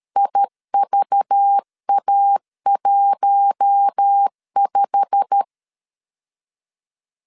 Morse code, heard as a keyed tone, about 13 wpm, 800 Hz. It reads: IVA15
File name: CW / DET / 1 9 1500C S A 12wpm 3800